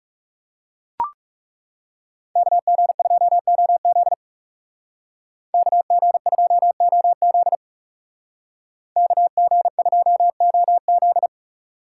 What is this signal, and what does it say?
Morse code, heard as a keyed tone, about 38 wpm, 700 Hz. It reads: KG2OZ KG2OZ KG2OZ